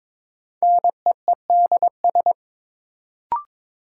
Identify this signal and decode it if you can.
Morse code, heard as a keyed tone, about 22 wpm, 700 Hz. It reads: NEEDS
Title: Sentences from Top 1000 Words - Review Part 3 - New Word Drill - 22wpm